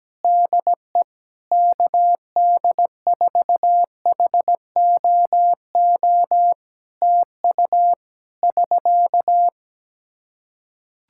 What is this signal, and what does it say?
Morse code, heard as a keyed tone, about 17 wpm, 700 Hz. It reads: DE KD4HOO TU <SK>
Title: Practice QSO - Exchange 4 - Ending - 17wpm